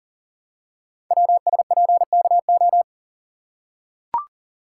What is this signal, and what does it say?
Morse code, heard as a keyed tone, about 41 wpm, 700 Hz. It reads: WSPKO
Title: Code Groups - 5 Letters - 40wpm